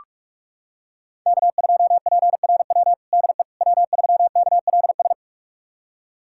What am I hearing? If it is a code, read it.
Morse code, heard as a keyed tone, about 46 wpm, 700 Hz. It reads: K2PRW DE W3KLS
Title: USA Callsign DE Callsign - 5 Characters - 45wpm